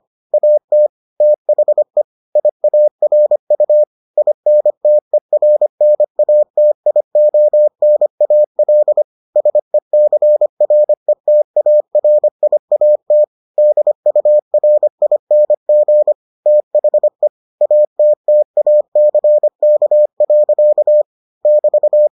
Morse code, heard as a keyed tone, about 25 wpm, 600 Hz. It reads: AT THE IARU INTERNATIONAL SECRETARIAT DURING THE ATTACK. <BT>